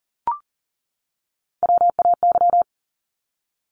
Morse code, heard as a keyed tone, about 40 wpm, 690 Hz. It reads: WAY
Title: Top Three Letter Combinations in Words 40wpm